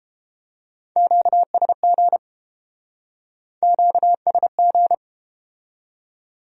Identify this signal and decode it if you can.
Morse code, heard as a keyed tone, about 31 wpm, 700 Hz. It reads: QSG QSG